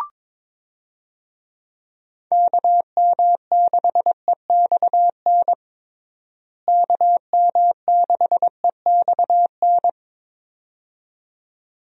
Morse code, heard as a keyed tone, about 22 wpm, 700 Hz. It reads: KM6EXN KM6EXN
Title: USA Call Signs - 6 Characters - Repeated 2x - 22wpm